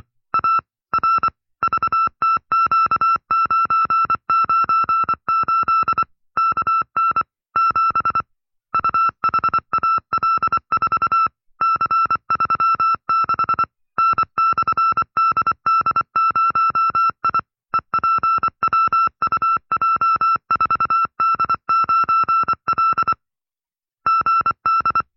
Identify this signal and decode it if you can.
Morse code, heard as a keyed tone, about 24 wpm, 1340 Hz. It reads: A R VTQ998 KN 7 UHAL4 C36 N/DD0I EPWUJ4D9L GD
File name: FreeFiles / CW-practice / 20WPM-side2